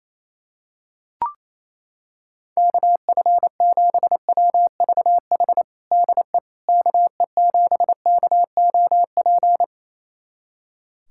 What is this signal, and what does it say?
Morse code, heard as a keyed tone, about 28 wpm, 700 Hz. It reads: KF7WVH DE KE7KOP
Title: USA Callsign DE Callsign - 6 Characters - 28wpm